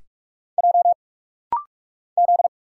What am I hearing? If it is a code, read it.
Morse code, heard as a keyed tone, about 46 wpm, 700 Hz. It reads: J Z
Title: Single Letter-Number - Rapid-Fire 45wpm